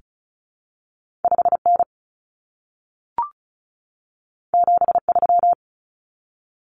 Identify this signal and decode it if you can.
Morse code, heard as a keyed tone, about 36 wpm, 700 Hz. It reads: 5N 73
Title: Common Cut Numbers - 2-Digit 35wpm